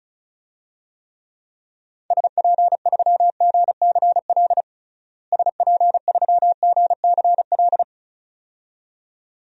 Morse code, heard as a keyed tone, about 34 wpm, 700 Hz. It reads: SP3GCL SP3GCL
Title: DXCC-100 Call Signs - 6 Characters - Repeated 2x - 35wpm